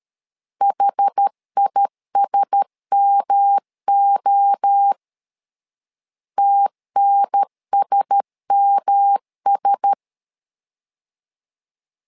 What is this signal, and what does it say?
Morse code, heard as a keyed tone, about 13 wpm, 800 Hz. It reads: HISMO TNSMS
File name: LEZIONI / 4 Q1S 12wpm 1802